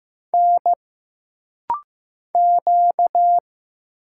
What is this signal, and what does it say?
Morse code, heard as a keyed tone, about 15 wpm, 700 Hz. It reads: N Q
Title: Single Letter-Number Alt - Rapid-Fire 15wpm